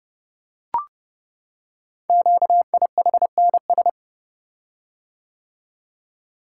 Morse code, heard as a keyed tone, about 30 wpm, 700 Hz. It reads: QIHNS
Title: Code Groups - 5 Letters - 30wpm